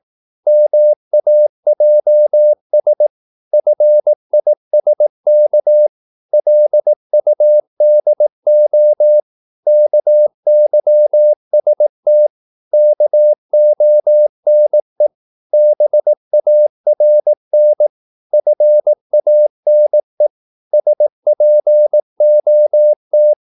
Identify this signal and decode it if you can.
Morse code, heard as a keyed tone, about 18 wpm, 600 Hz. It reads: MAJS FISK LUDO KYST KONE BARN FANE SPOT
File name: Korte ord DK 18wpm